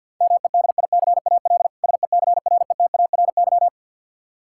Morse code, heard as a keyed tone, about 52 wpm, 700 Hz. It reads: MEDICAL SECRETARY
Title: Practice QSO - Exchange 3 - More Info - 50wpm